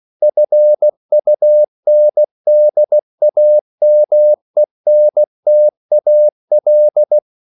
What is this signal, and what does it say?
Morse code, heard as a keyed tone, about 16 wpm, 600 Hz. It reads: FUNDAMENTAL
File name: CW / Practice / EnglishQuotations0116